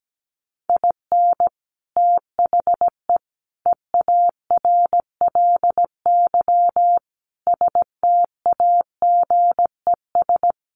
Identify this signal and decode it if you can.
Morse code, heard as a keyed tone, about 17 wpm, 700 Hz. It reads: IN THE EARLY STAGES
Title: Sets of 5 Words Encore 17wpm